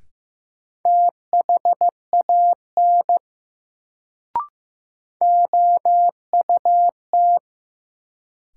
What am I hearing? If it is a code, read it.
Morse code, heard as a keyed tone, about 15 wpm, 700 Hz. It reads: THAN OUT
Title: Sets of 2 Words - New Word Drill - 15wpm